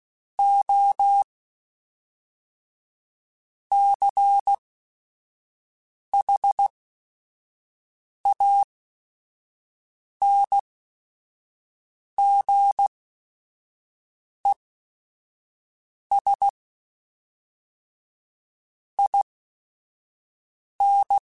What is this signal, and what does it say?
Morse code, heard as a keyed tone, about 16 wpm, spaced as for 5 wpm, 800 Hz. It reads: O CHANGES IN
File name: Technical Subjects / Technical Subjects-mp3s / Reactance @5